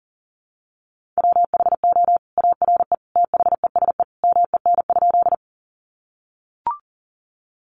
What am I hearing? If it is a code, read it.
Morse code, heard as a keyed tone, about 40 wpm, 700 Hz. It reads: WHO ARE THESE MEN?
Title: Sentences from Top 200 Words 40wpm